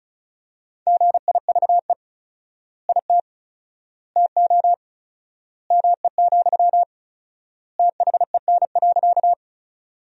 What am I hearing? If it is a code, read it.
Morse code, heard as a keyed tone, about 34 wpm, 700 Hz. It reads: GIVE IT TO ME, THEN.
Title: 4x Word-Spacing - Sentences from Top 200 Words - 35wpm